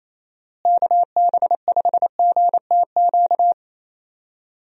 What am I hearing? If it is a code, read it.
Morse code, heard as a keyed tone, about 28 wpm, 700 Hz. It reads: KB5GTQ
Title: USA Call Signs - 6 Characters - Repeated 2x - 28wpm